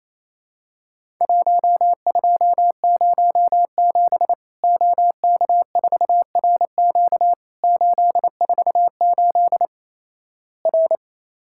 Morse code, heard as a keyed tone, about 28 wpm, 700 Hz. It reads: 1207 OK4RQ 848 R
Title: Worked All Europe DX Contest - QTC Exchange - 28wpm